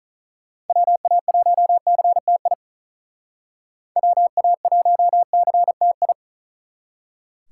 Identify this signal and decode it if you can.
Morse code, heard as a keyed tone, about 44 wpm, 700 Hz. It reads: WA1CTI WA1CTI
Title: Speed Racing - USA Call Signs 35wpm